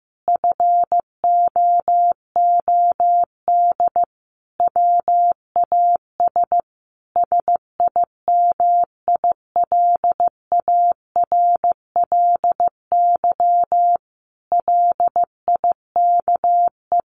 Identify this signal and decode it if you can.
Morse code, heard as a keyed tone, about 15 wpm, 700 Hz. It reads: FOOD WAS SIMILARLY LIKE
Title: Sentences from Top 1400 Words 15wpm